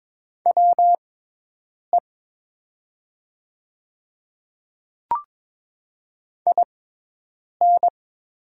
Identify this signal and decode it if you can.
Morse code, heard as a keyed tone, about 22 wpm, spaced as for 8 wpm, 700 Hz. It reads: WE IN
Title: 6x Character Spacing - Top 200 Words - 22wpm